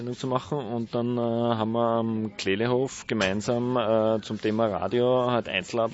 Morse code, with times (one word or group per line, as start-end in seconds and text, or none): none